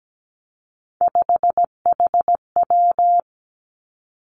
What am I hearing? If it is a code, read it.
Morse code, heard as a keyed tone, about 17 wpm, 700 Hz. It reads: 5HW
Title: Three Challenging Letter-Number Combinations 17wpm